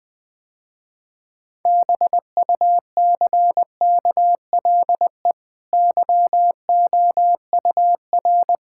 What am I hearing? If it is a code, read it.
Morse code, heard as a keyed tone, about 21 wpm, 700 Hz. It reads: BUCKLE YOUR
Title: Easy Sentences with Hints - 20wpm